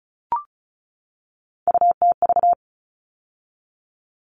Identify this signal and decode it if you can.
Morse code, heard as a keyed tone, about 34 wpm, 700 Hz. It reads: UTV
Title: All Cut Numbers - 3-Digit 35wpm